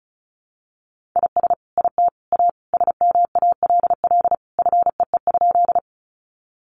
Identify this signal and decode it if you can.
Morse code, heard as a keyed tone, about 34 wpm, 700 Hz. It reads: IS IT A SMALL FEE?
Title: Sentences from Top 1500 Words 35wpm